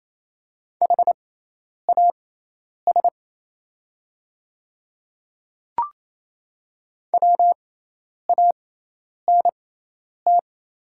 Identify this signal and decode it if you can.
Morse code, heard as a keyed tone, about 28 wpm, spaced as for 10 wpm, 700 Hz. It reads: HAS WANT
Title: 6x Character Spacing - Top 200 Words - 28wpm